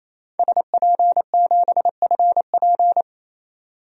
Morse code, heard as a keyed tone, about 28 wpm, 700 Hz. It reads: SP7FP
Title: DXCC-100 Call Signs - 5 Characters - 28wpm